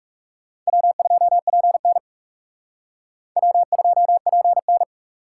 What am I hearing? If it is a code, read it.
Morse code, heard as a keyed tone, about 38 wpm, 700 Hz. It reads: W2PN W2PN